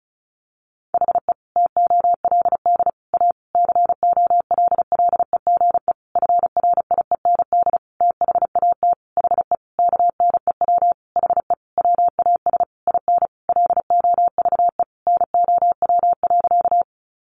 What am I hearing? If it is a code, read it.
Morse code, heard as a keyed tone, about 34 wpm, 700 Hz. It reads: HE TOLD A COLLEGE FRIEND THAT HE KNEW HE WAS IN LOVE NOW.
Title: Sentences from Top 1300 Words 35wpm